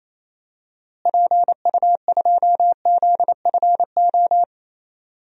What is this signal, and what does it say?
Morse code, heard as a keyed tone, about 28 wpm, 700 Hz. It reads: PU2ZFO
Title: DXCC-100 Call Signs - 6 Characters - Repeated 2x - 28wpm